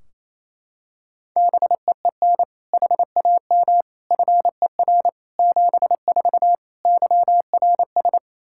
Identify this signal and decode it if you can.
Morse code, heard as a keyed tone, about 28 wpm, 700 Hz. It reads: BEEN HAM FER 74 YRS
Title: K1USN SST QSO Practice 28wpm